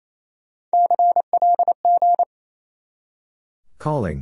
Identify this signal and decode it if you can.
Morse code, heard as a keyed tone, about 37 wpm, 700 Hz. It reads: CLG